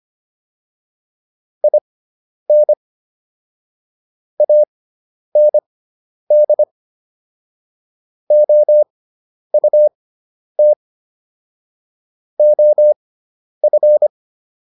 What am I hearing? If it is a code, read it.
Morse code, heard as a keyed tone, about 25 wpm, spaced as for 10 wpm, 600 Hz. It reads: IN AND OUT OF
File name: WD304 10